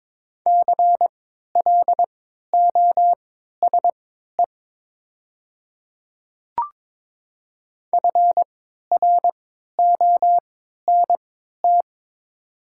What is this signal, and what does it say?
Morse code, heard as a keyed tone, about 22 wpm, spaced as for 13 wpm, 700 Hz. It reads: CLOSE FRONT
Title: 3x Character Spacing - Top 500 Words - 22wpm